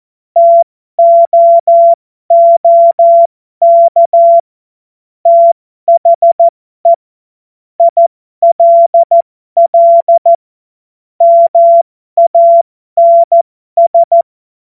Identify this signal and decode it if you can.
Morse code, heard as a keyed tone, about 14 wpm, spaced as for 12 wpm, 680 Hz. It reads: TOOK THE ILL MANS